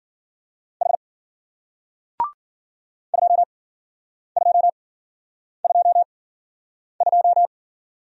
Morse code, heard as a keyed tone, about 51 wpm, spaced as for 9 wpm, 700 Hz. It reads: H 2222